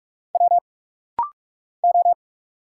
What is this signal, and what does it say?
Morse code, heard as a keyed tone, about 46 wpm, 700 Hz. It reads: W O
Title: Single Letters - Rapid-Fire 45wpm